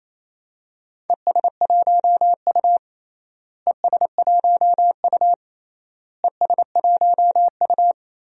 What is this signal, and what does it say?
Morse code, heard as a keyed tone, about 28 wpm, 700 Hz. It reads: ES1U ES1U ES1U
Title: DXCC-100 Call Signs - 4 Characters - Repeated 3x - 28wpm